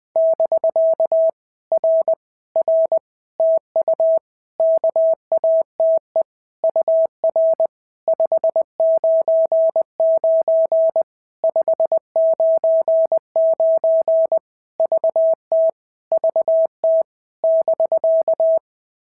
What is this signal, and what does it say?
Morse code, heard as a keyed tone, about 20 wpm, 650 Hz. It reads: <BK> R R TU KATE UR 599 599 VT VT <BK>
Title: POTA QSO Practice - 20wpm